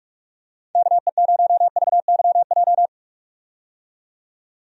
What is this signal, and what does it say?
Morse code, heard as a keyed tone, about 44 wpm, 700 Hz. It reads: KE0VYJ
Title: USA Call Signs - 6 Characters - 45wpm